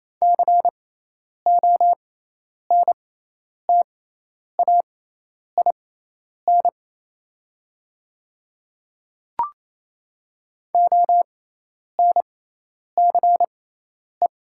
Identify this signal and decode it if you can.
Morse code, heard as a keyed tone, about 28 wpm, spaced as for 10 wpm, 700 Hz. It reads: CONTAIN ONCE
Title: 6x Character Spacing - Top 500 Words - 28wpm